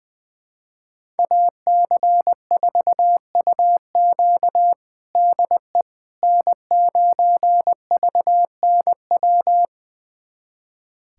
Morse code, heard as a keyed tone, about 20 wpm, 700 Hz. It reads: AC4UQ DE N9VNW
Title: USA Callsign DE Callsign - 5 Characters - 20wpm